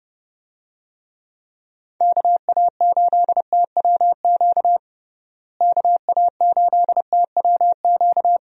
Morse code, heard as a keyed tone, about 30 wpm, 700 Hz. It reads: KA8TWQ KA8TWQ
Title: USA Call Signs - 6 Characters - Repeated 3x - 30wpm